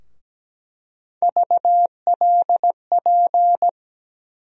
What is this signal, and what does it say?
Morse code, heard as a keyed tone, about 17 wpm, 700 Hz. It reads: VLP
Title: USA Call Sign Postfixes - 3 Letters 17wpm